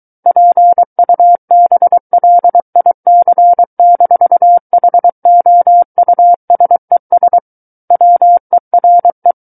Morse code, heard as a keyed tone, about 23 wpm, 700 Hz. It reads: PUBLIC-HOUSES WERE
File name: Book One / 14-Chapter 14 - 23 WPM